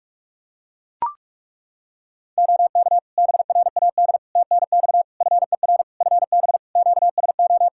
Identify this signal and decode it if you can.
Morse code, heard as a keyed tone, about 43 wpm, 700 Hz. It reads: OK BRAD TNX FER FB QSO
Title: K1USN SST QSO Practice 45wpm